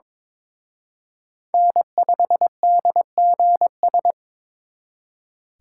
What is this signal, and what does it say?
Morse code, heard as a keyed tone, about 26 wpm, 700 Hz. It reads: N5DGS
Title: Speed Racing - USA 1x3 Call Signs 22wpm